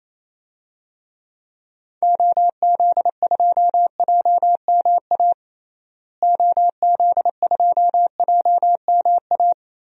Morse code, heard as a keyed tone, about 28 wpm, 700 Hz. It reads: OZ2JMA OZ2JMA
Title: DXCC-100 Call Signs - 6 Characters - Repeated 3x - 28wpm